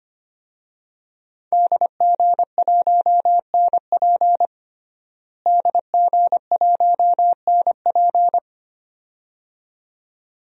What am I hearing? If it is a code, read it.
Morse code, heard as a keyed tone, about 25 wpm, 700 Hz. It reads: DG1NP DG1NP
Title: DXCC-100 Call Signs - 5 Characters - Repeated 2x - 25wpm